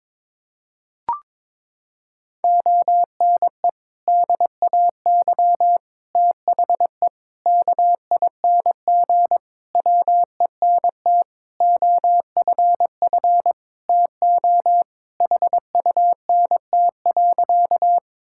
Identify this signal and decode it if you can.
Morse code, heard as a keyed tone, about 22 wpm, 700 Hz. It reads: ONE DAY THE KING WENT OFF TO HUNT.
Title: Sentences from Top 1600 Words 22wpm